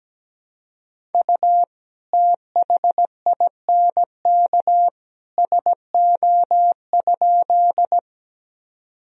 Morse code, heard as a keyed tone, about 17 wpm, 700 Hz. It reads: U THINK SO?
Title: Sentences from Top 200 Words and Abbreviations 17wpm